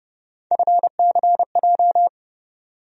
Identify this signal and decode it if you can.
Morse code, heard as a keyed tone, about 30 wpm, 700 Hz. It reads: FCJ
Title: USA Call Sign Postfixes - 3 Letters 30wpm